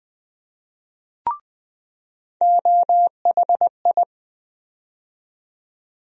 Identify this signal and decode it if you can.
Morse code, heard as a keyed tone, about 20 wpm, 700 Hz. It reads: OHI
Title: USA Call Sign Postfixes - 3 Letters 20wpm